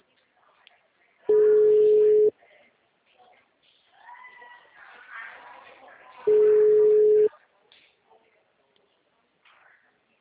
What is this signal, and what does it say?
Ringback tone (Europe, 425 Hz), with speech.